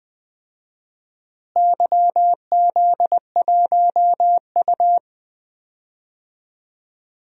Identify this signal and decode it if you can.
Morse code, heard as a keyed tone, about 20 wpm, 700 Hz. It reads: YZ1U
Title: DXCC-100 Call Signs - 4 Characters - Repeated 2x - 20wpm